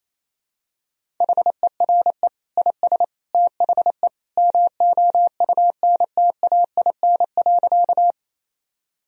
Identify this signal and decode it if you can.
Morse code, heard as a keyed tone, about 28 wpm, 700 Hz. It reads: HERE IS THE MOUNTAIN.
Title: Sentences from Top 1000 Words - Review Part 1 - 28wpm